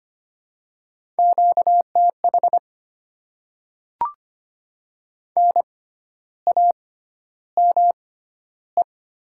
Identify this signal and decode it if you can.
Morse code, heard as a keyed tone, about 25 wpm, spaced as for 12 wpm, 690 Hz. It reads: QTH NAME
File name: Top 20 QSO Elements - 6x Character Spacing - 25wpm